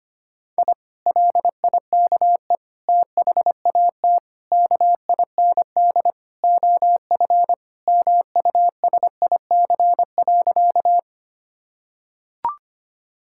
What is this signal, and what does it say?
Morse code, heard as a keyed tone, about 25 wpm, 700 Hz. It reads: I LIKE THAT KIND OF MUSIC.
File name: practice / 25wpm / Sentences with 3-Word Phrases - 25wpm